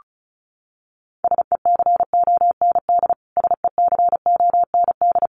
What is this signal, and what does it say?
Morse code, heard as a keyed tone, about 34 wpm, 700 Hz. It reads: SECOND SECOND
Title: Top 500 Words Repeated 35wpm